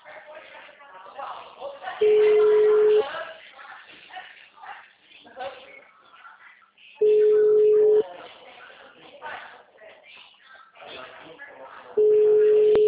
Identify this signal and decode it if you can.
Ringback tone (Europe, 425 Hz), with speech.